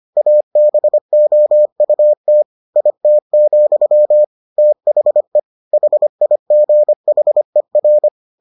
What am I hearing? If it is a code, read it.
Morse code, heard as a keyed tone, about 25 wpm, 600 Hz. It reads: ABOUT IT, THE HIGHER